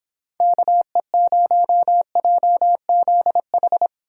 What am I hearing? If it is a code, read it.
Morse code, heard as a keyed tone, about 25 wpm, 700 Hz. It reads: KE0JZH